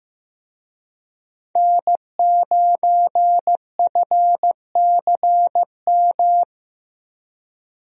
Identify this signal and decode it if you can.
Morse code, heard as a keyed tone, about 15 wpm, 700 Hz. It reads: N9FCM
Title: USA Call Signs - 5 Characters - Repeated 2x - 15wpm